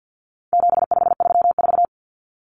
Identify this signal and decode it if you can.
Morse code, heard as a keyed tone, about 50 wpm, 700 Hz. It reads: 7534
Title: Common Cut Numbers - 4-Digit 50wpm